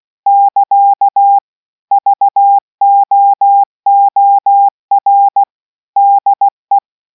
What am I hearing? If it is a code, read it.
Morse code, heard as a keyed tone, about 16 wpm, 800 Hz. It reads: <KA> VOOR DE